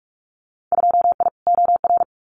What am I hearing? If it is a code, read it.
Morse code, heard as a keyed tone, about 46 wpm, 700 Hz. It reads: 2I OR
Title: Winter Field Day Exchanges Encore 45wpm